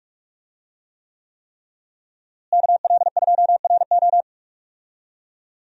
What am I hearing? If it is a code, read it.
Morse code, heard as a keyed tone, about 41 wpm, 700 Hz. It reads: KL2RO
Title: USA Call Signs - 5 Characters - Repeated 3x - 45wpm